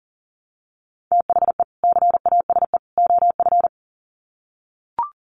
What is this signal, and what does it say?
Morse code, heard as a keyed tone, about 40 wpm, 700 Hz. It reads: THE CASE OF
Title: Sets of 3 Words 40wpm